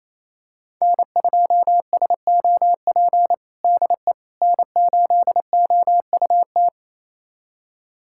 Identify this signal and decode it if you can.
Morse code, heard as a keyed tone, about 27 wpm, 700 Hz. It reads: N2SOP DE N8OUT
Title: USA Callsign DE Callsign - 5 Characters - 28wpm